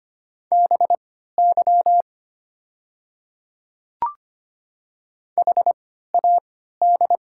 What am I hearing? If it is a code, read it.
Morse code, heard as a keyed tone, about 25 wpm, spaced as for 14 wpm, 690 Hz. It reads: BY HAD
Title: 3x Character Spacing - Top 100 Words - 25wpm